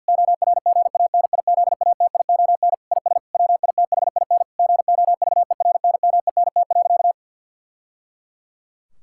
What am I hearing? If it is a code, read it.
Morse code, heard as a keyed tone, about 50 wpm, 700 Hz. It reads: ORGANIZATION IS WITHIN GOVERNMENT.
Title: Sentences from Top 1000 Words - Review Part 5 - 50wpm